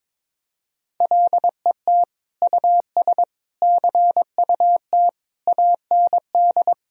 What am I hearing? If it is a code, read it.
Morse code, heard as a keyed tone, about 22 wpm, 700 Hz. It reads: LET US CUT AND
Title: Sentences from Top 1000 Words - Review Part 1 - 22wpm